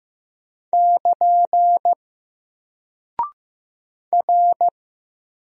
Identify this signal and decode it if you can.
Morse code, heard as a keyed tone, about 15 wpm, 700 Hz. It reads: <KN> R
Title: Single Character (Letter:Number:Punctuation:Prosign) Alt 15wpm